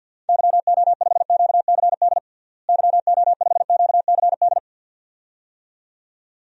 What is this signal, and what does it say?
Morse code, heard as a keyed tone, about 50 wpm, 700 Hz. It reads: YO5QCD YO5QCD
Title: DXCC-100 Call Signs - 6 Characters - Repeated 2x - 50wpm